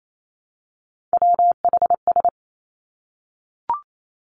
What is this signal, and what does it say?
Morse code, heard as a keyed tone, about 28 wpm, 700 Hz. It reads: WHS
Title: Three Challenging Letter-Number Combinations 28wpm